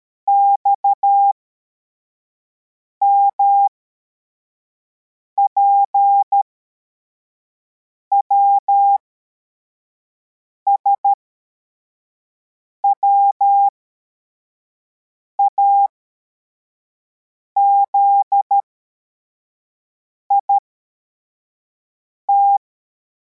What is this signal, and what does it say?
Morse code, heard as a keyed tone, about 13 wpm, spaced as for 4 wpm, 800 Hz. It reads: XMPWS WAZIT